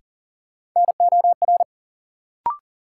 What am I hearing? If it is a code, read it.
Morse code, heard as a keyed tone, about 40 wpm, 700 Hz. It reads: NOR
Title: Sentences from Top 1500 Words - New Word Drill - 40wpm